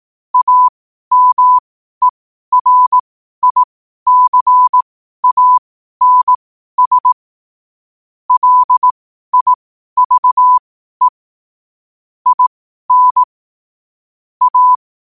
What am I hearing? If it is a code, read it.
Morse code, heard as a keyed tone, about 18 wpm, spaced as for 12 wpm, 1000 Hz. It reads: AMERICANS LIVE IN A
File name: qotdfiles / 12wpm